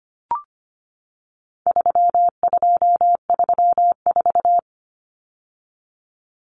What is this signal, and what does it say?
Morse code, heard as a keyed tone, about 25 wpm, 700 Hz. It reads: 3234
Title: Common Cut Numbers - 4-Digit 25wpm